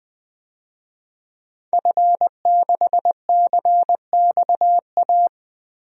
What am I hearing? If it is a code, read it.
Morse code, heard as a keyed tone, about 21 wpm, 700 Hz. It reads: F6CXA
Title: DXCC-100 Call Signs - 5 Characters - Repeated 3x - 20wpm